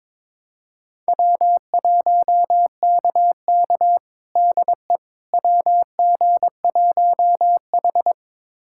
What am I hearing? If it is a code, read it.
Morse code, heard as a keyed tone, about 21 wpm, 700 Hz. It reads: W1KK DE WG1H
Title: USA Callsign DE Callsign - 4 Characters - 22wpm